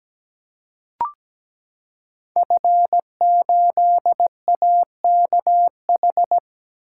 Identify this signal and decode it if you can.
Morse code, heard as a keyed tone, about 17 wpm, 700 Hz. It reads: F8AKH
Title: Code Groups - 5 Letter:Numbers - 17wpm